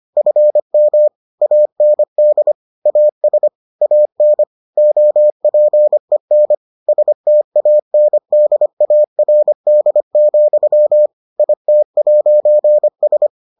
Morse code, heard as a keyed tone, about 25 wpm, 600 Hz. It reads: FM AND AS AN OPEN STANDARD, IT'S